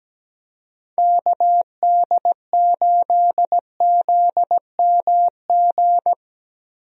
Morse code, heard as a keyed tone, about 17 wpm, 700 Hz. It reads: KD8ZMG
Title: USA Call Signs - 6 Characters - Repeated 2x - 17wpm